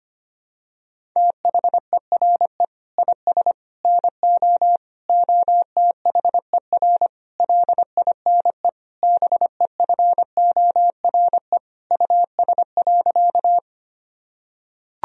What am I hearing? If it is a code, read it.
Morse code, heard as a keyed tone, about 25 wpm, 700 Hz. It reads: THERE IS NO OTHER LINE BEFORE US.
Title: Sentences from Top 1000 Words - Review Part 1 - 25wpm